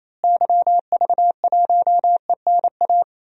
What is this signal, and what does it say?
Morse code, heard as a keyed tone, about 27 wpm, 700 Hz. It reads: YV1ENA
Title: DXCC-100 Call Signs - 6 Characters - Repeated 2x - 28wpm